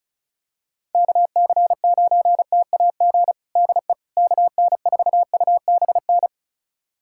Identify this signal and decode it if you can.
Morse code, heard as a keyed tone, about 34 wpm, 700 Hz. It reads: KC9TAG DE KN4UBN
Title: USA Callsign DE Callsign - 6 Characters - 35wpm